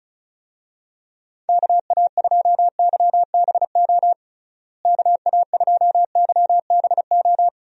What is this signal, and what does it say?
Morse code, heard as a keyed tone, about 34 wpm, 700 Hz. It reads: KA2YBO KA2YBO
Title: USA Call Signs - 6 Characters - Repeated 2x - 35wpm